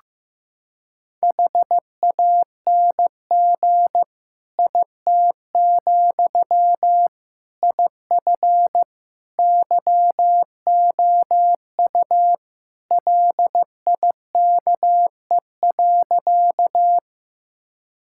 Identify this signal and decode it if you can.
Morse code, heard as a keyed tone, about 15 wpm, 700 Hz. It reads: HANG IT, IF YOU LIKE.
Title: Sentences from Top 1600 Words 15wpm